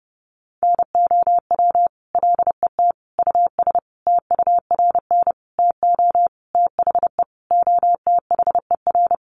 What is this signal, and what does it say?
Morse code, heard as a keyed tone, about 30 wpm, 700 Hz. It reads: NOW LET US TURN TO THE OTHER
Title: Sentences from Top 1400 Words 30wpm